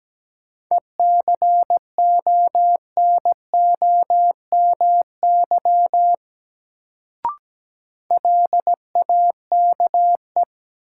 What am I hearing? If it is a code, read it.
Morse code, heard as a keyed tone, about 17 wpm, 700 Hz. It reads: ECONOMY LAKE
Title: Sentences from Top 1000 Words - Review Part 3 - New Word Drill - 17wpm